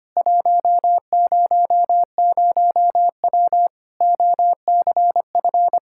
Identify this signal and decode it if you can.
Morse code, heard as a keyed tone, about 25 wpm, 700 Hz. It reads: 100W OCF
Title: Practice QSO - Exchange 2 - Info - 25wpm